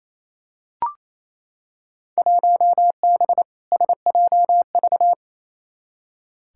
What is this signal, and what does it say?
Morse code, heard as a keyed tone, about 28 wpm, 700 Hz. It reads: 1B SJV
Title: Field Day Exchanges Encore - 28wpm